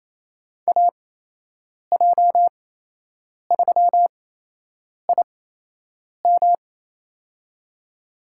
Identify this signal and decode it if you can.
Morse code, heard as a keyed tone, about 28 wpm, spaced as for 8 wpm, 700 Hz. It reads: AJ3IM
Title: USA Call Signs - 8x Character Spacing - 28wpm